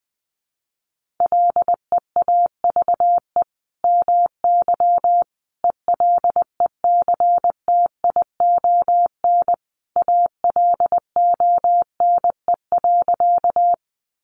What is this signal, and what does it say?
Morse code, heard as a keyed tone, about 20 wpm, 700 Hz. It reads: LEAVE MY ELECTION ALONE.
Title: Sentences from Top 1500 Words 20wpm